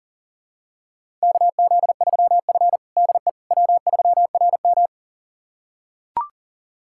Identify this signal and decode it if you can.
Morse code, heard as a keyed tone, about 37 wpm, 700 Hz. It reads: KZ3F DE W3RM
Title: USA Callsign DE Callsign - 4 Characters - 40wpm